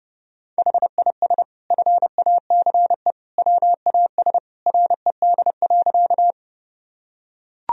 Morse code, heard as a keyed tone, about 30 wpm, 700 Hz. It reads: HIS FACE WAS RED.
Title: Sentences from Top 1000 Words - Review Part 2 - 30wpm